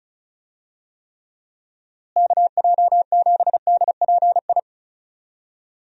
Morse code, heard as a keyed tone, about 34 wpm, 700 Hz. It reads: KJ7DPI
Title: USA Call Signs - 6 Characters - Repeated 3x - 35wpm